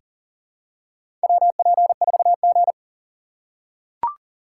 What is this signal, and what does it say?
Morse code, heard as a keyed tone, about 40 wpm, 700 Hz. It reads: WP4G
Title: USA Call Signs - 4 Characters - 6x Character Spacing - 40wpm